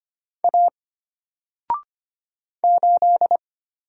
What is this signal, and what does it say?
Morse code, heard as a keyed tone, about 25 wpm, 700 Hz. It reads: A 8
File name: Single-Letter-Number-25wpm